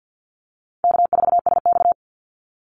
Morse code, heard as a keyed tone, about 48 wpm, 700 Hz. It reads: K4SX
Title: Easy Call Signs Repeated 2x - 50wpm